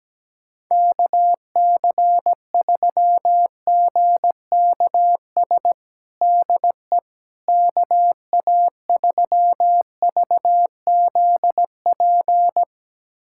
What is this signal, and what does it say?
Morse code, heard as a keyed tone, about 17 wpm, 700 Hz. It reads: KC3GKS DE KA3VZP
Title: USA Callsign DE Callsign - 6 Characters - 17wpm